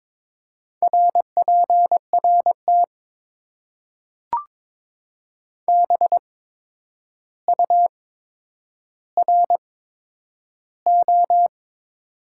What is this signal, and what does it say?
Morse code, heard as a keyed tone, about 22 wpm, spaced as for 9 wpm, 700 Hz. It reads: RPRT BURO
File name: Top 90 QSO Elements - 8x Character Spacing - 22wpm